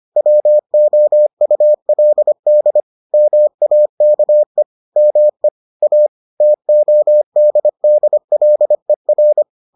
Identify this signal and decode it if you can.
Morse code, heard as a keyed tone, about 25 wpm, 600 Hz. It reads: WOULD MAKE ME A TODDLER